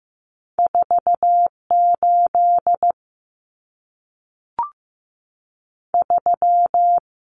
Morse code, heard as a keyed tone, about 15 wpm, 700 Hz. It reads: 48 3
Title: Common Cut Numbers - 2-Digit 15wpm